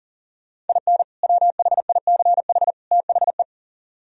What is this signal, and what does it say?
Morse code, heard as a keyed tone, about 49 wpm, 700 Hz. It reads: IN WHICH THE
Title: Speed-Racing - Sets of 3 Words 40wpm